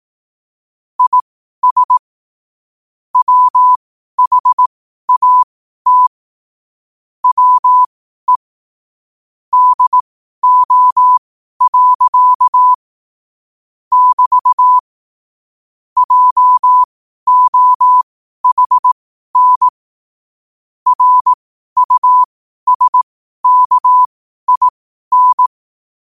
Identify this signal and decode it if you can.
Morse code, heard as a keyed tone, about 18 wpm, spaced as for 12 wpm, 1000 Hz. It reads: IS WHAT WE DO. <BT> JOHN RUSKIN
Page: New quotes every day in morse code at 12 Words per minute.